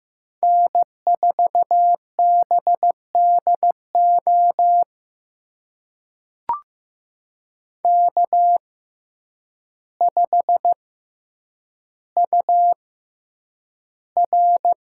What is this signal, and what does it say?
Morse code, heard as a keyed tone, about 15 wpm, spaced as for 8 wpm, 700 Hz. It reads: N4BDO K5UR
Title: USA Call Signs - 5 Characters - 6x Character Spacing - 15wpm